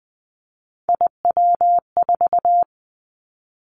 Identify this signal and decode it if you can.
Morse code, heard as a keyed tone, about 20 wpm, 700 Hz. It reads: IW4
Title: Callsign Prefixes for Top 25 DXCC - 20wpm